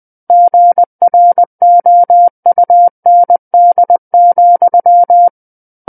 Morse code, heard as a keyed tone, about 20 wpm, 700 Hz. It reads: GROUND,
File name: 20 WPM